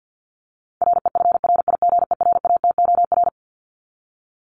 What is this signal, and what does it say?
Morse code, heard as a keyed tone, about 50 wpm, 700 Hz. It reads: REFRIGERATOR
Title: Top 1501-1750 Words 50wpm